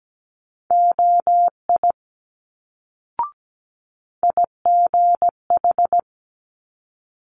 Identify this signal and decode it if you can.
Morse code, heard as a keyed tone, about 17 wpm, 700 Hz. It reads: OI IGH
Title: Multi-letter Phonemes Quick Fire 17wpm